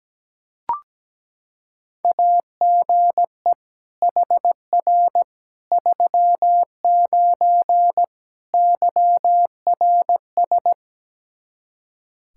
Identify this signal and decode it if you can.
Morse code, heard as a keyed tone, about 17 wpm, 700 Hz. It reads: AGE HR 39 YRS
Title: Practice QSO - Exchange 3 - More Info - 17wpm